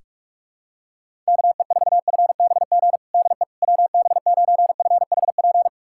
Morse code, heard as a keyed tone, about 46 wpm, 700 Hz. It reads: KE4FBG DE WB9FHP
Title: USA Callsign DE Callsign - 6 Characters - 45wpm